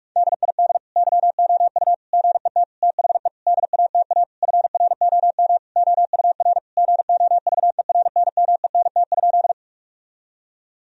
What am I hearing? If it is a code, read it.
Morse code, heard as a keyed tone, about 41 wpm, 700 Hz. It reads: DID YOU GET THE DATA FROM OUR GOVERNMENT?
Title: Sentences from Top 1000 Words - Review Part 3 - 45wpm